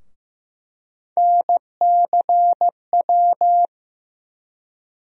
Morse code, heard as a keyed tone, about 15 wpm, 700 Hz. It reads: NCW
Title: USA Call Sign Postfixes - 3 Letters 15wpm